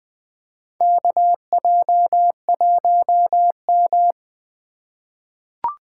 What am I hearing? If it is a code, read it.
Morse code, heard as a keyed tone, about 20 wpm, 700 Hz. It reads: KJ1M
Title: USA Call Signs - 4 Characters - 8x Character Spacing - 20wpm